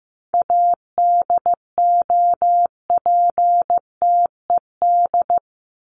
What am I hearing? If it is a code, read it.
Morse code, heard as a keyed tone, about 15 wpm, 700 Hz. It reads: ADOPTED
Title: ED Suffix - Top 100 Words with Base Words 15wpm